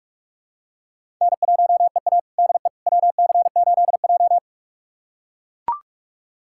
Morse code, heard as a keyed tone, about 41 wpm, 700 Hz. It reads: N1EA DE WC8J
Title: USA Callsign DE Callsign - 4 Characters - 45wpm